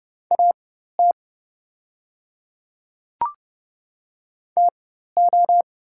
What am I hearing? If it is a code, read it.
Morse code, heard as a keyed tone, about 30 wpm, spaced as for 14 wpm, 700 Hz. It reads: AT TO
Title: 4x Character Spacing - Top 200 Words - 30wpm